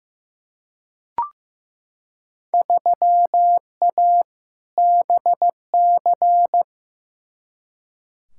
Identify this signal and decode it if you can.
Morse code, heard as a keyed tone, about 16 wpm, 700 Hz. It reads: 3A BC
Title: Field Day Exchanges - 15wpm